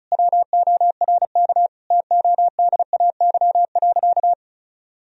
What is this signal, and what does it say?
Morse code, heard as a keyed tone, about 36 wpm, 700 Hz. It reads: WORK TODAY.
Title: Sentences from Top 1000 Words - Review Part 4 - 35wpm